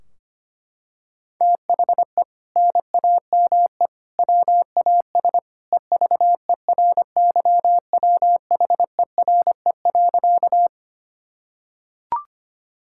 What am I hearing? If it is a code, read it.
Morse code, heard as a keyed tone, about 25 wpm, 700 Hz. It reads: THE NAME WAS EVERYWHERE.
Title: Sentences from Top 1600 Words 25wpm